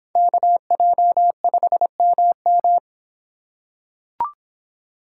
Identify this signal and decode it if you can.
Morse code, heard as a keyed tone, about 25 wpm, 700 Hz. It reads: KJ5MM